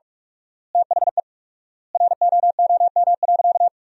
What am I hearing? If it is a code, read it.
Morse code, heard as a keyed tone, about 42 wpm, 700 Hz. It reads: THE ROOM.
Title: 4x Word-Spacing - Sentences from Top 300 Words - 45wpm